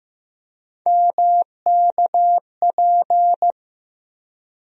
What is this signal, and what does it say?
Morse code, heard as a keyed tone, about 15 wpm, 700 Hz. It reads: MKP
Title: USA Call Sign Postfixes - 3 Letters 15wpm